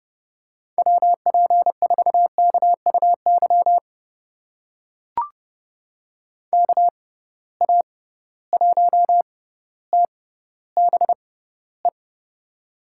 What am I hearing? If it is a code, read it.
Morse code, heard as a keyed tone, about 30 wpm, spaced as for 15 wpm, 700 Hz. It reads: WP4KUY KA1TBE
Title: USA Call Signs - 6 Characters - 6x Character Spacing - 30wpm